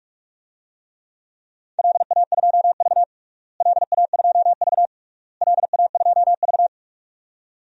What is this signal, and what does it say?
Morse code, heard as a keyed tone, about 46 wpm, 700 Hz. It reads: LA2V LA2V LA2V
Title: DXCC-100 Call Signs - 4 Characters - Repeated 3x - 45wpm